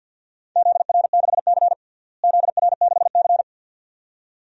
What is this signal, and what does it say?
Morse code, heard as a keyed tone, about 51 wpm, 700 Hz. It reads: ZR6C ZR6C
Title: DXCC-100 Call Signs - 4 Characters - Repeated 2x - 50wpm